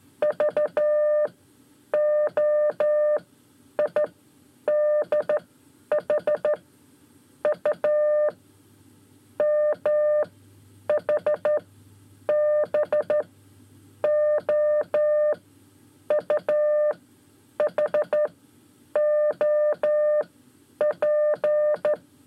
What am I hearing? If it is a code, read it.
Morse code, heard as a keyed tone, about 12 wpm, 600 Hz. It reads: V O I D H U M H B O U H O P
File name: 5 Lezione CW 1